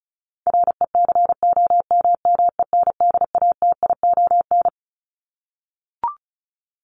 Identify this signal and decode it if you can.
Morse code, heard as a keyed tone, about 34 wpm, 700 Hz. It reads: RECOMMENDATION
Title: Top 1251-1500 Words 35wpm